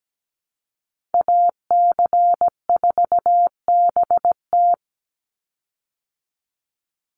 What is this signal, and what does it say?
Morse code, heard as a keyed tone, about 17 wpm, 700 Hz. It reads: AC4BT
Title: Call Signs Repeated 2x - 17wpm